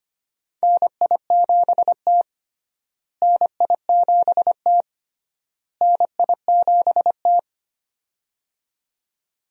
Morse code, heard as a keyed tone, about 25 wpm, 700 Hz. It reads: NI7T NI7T NI7T
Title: USA Call Signs - 4 Characters - Repeated 3x - 25wpm